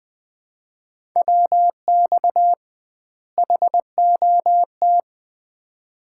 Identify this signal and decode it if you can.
Morse code, heard as a keyed tone, about 20 wpm, 700 Hz. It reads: WX HOT
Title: Common QSO Phrases - 2x Word-Spacing - 20wpm